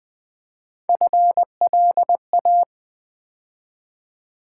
Morse code, heard as a keyed tone, about 20 wpm, 700 Hz. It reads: FLA
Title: Build A Word - 3 Letters At A Time - 6-Letter Words - 20wpm